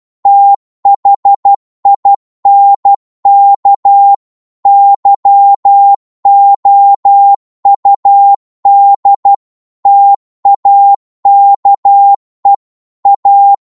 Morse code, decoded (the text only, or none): THINK YOUD TAKE A